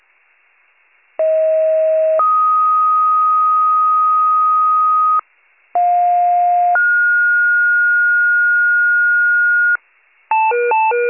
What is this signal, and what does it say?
Two-tone page, heard as a one-second tone, then a three-second tone, about 640 then 1250 Hz, about 710 then 1430 Hz, with noise. It is heard from 1.2 to 9.8 s.